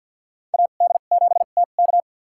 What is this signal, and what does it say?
Morse code, heard as a keyed tone, about 52 wpm, spaced as for 36 wpm, 700 Hz. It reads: AD7TK
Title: USA Call Signs - 5 Characters - 2x Character Spacing - 50wpm